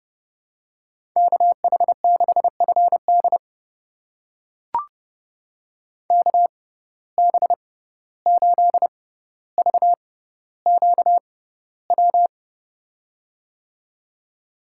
Morse code, heard as a keyed tone, about 32 wpm, spaced as for 15 wpm, 700 Hz. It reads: KH6FD KB8VQW
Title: USA Call Signs - 6x Character Spacing - 30wpm